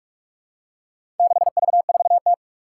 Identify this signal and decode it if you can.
Morse code, heard as a keyed tone, about 46 wpm, 700 Hz. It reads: 6V4T
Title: Four Challenging Letter-Number Combinations 45wpm